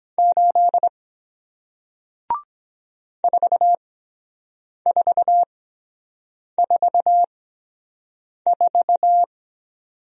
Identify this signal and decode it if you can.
Morse code, heard as a keyed tone, about 23 wpm, spaced as for 7 wpm, 700 Hz. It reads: 8 4444